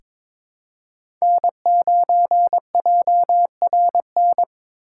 Morse code, heard as a keyed tone, about 21 wpm, 700 Hz. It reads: N9JRN
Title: USA Call Signs - 4x Character Spacing - 22wpm